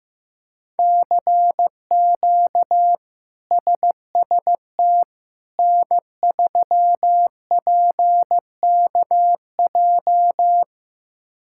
Morse code, heard as a keyed tone, about 16 wpm, 700 Hz. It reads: CQ SST N3PKJ
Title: K1USN SST QSO Practice - 15wpm